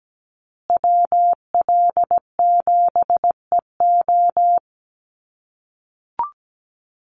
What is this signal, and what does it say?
Morse code, heard as a keyed tone, about 17 wpm, 700 Hz. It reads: WL7EO
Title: USA Call Signs Repeated 3x 17wpm